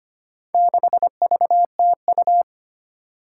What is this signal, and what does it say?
Morse code, heard as a keyed tone, about 25 wpm, 700 Hz. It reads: 6VTU
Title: Four Challenging Letter-Number Combinations 25wpm